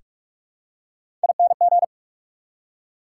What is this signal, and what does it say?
Morse code, heard as a keyed tone, about 44 wpm, 690 Hz. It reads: ING
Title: Prefixes and Suffixes 45wpm